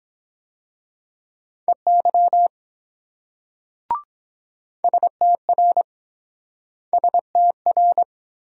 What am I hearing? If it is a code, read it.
Morse code, heard as a keyed tone, about 23 wpm, 700 Hz. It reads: EY STR STR